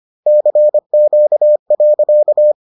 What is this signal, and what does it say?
Morse code, heard as a keyed tone, about 25 wpm, 600 Hz. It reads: CQ.